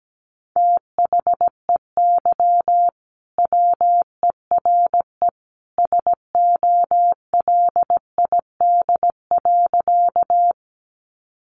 Morse code, heard as a keyed tone, about 17 wpm, 700 Hz. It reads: THEY WERE SOLID.
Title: Sentences from Top 1500 Words 17wpm